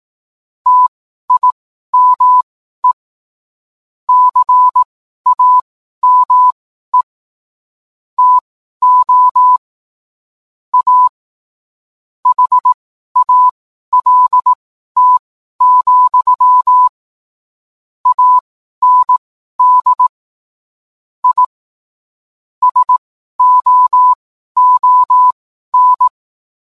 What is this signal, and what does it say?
Morse code, heard as a keyed tone, about 18 wpm, spaced as for 12 wpm, 1000 Hz. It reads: TIME CAME TO A HALT, AND I SOON